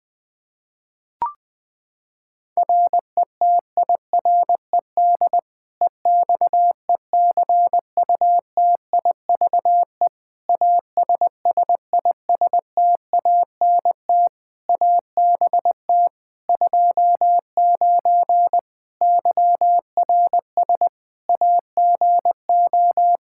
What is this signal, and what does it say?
Morse code, heard as a keyed tone, about 21 wpm, 700 Hz. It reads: RETIRED EXECUTIVE ASSISTANT ABT 29 YRS AGO
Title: Practice QSO - Exchange 3 - More Info - 20wpm